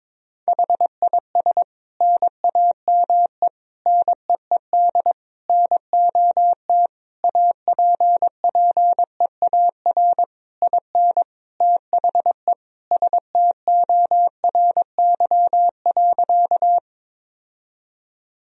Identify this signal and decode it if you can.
Morse code, heard as a keyed tone, about 22 wpm, 700 Hz. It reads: HIS NAME NEED NOT APPEAR IN THE STORY.
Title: Sentences from Top 1000 Words - Review Part 2 - 22wpm